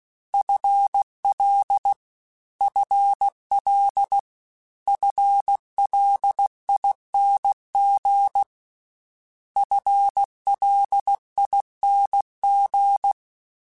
Morse code, heard as a keyed tone, about 16 wpm, 800 Hz. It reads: FL FL FLING FLING